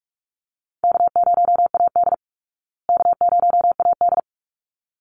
Morse code, heard as a keyed tone, about 44 wpm, 700 Hz. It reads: K0AD K0AD
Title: Call Signs Repeated 3x - 45wpm